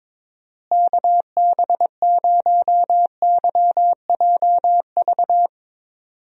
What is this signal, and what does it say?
Morse code, heard as a keyed tone, about 21 wpm, 700 Hz. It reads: KB0YJV
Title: USA Call Signs - 6 Characters - Repeated 2x - 22wpm